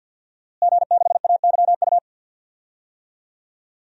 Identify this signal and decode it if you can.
Morse code, heard as a keyed tone, about 50 wpm, 700 Hz. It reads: G6AYU
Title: DXCC-100 Call Signs - 5 Characters - Repeated 3x - 50wpm